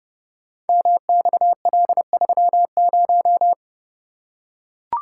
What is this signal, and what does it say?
Morse code, heard as a keyed tone, about 29 wpm, 700 Hz. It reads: MXL30
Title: Code Groups - 5 Letter:Numbers - 30wpm